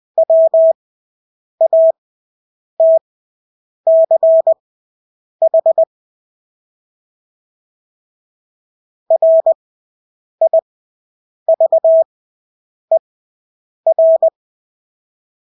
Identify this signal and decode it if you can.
Morse code, heard as a keyed tone, about 20 wpm, spaced as for 8 wpm, 650 Hz. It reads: WATCH RIVER
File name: Engelske ord 20/8 wpm